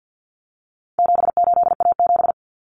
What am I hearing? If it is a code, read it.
Morse code, heard as a keyed tone, about 50 wpm, 700 Hz. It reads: 78A7
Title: Common Cut Numbers - 4-Digit 50wpm